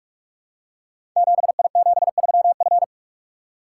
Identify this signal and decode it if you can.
Morse code, heard as a keyed tone, about 44 wpm, 700 Hz. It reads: 7I73F
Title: Code Groups - 5 Letter:Numbers - 45wpm